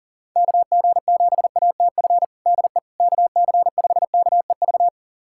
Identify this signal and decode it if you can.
Morse code, heard as a keyed tone, about 40 wpm, 700 Hz. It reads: KG7ATF DE KC5KEV
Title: USA Callsign DE Callsign - 6 Characters - 40wpm